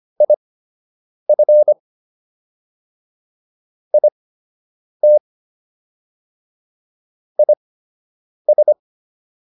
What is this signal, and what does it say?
Morse code, heard as a keyed tone, about 25 wpm, spaced as for 8 wpm, 600 Hz. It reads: IF IT IS